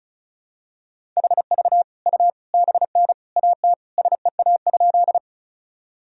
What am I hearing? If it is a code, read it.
Morse code, heard as a keyed tone, about 36 wpm, 700 Hz. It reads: HV U BN AT SEA?
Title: Sentences from Top 300 Words and Abbreviations 35wpm